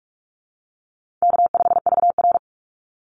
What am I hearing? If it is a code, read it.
Morse code, heard as a keyed tone, about 44 wpm, 700 Hz. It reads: K5VR
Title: Easy Call Signs - 45wpm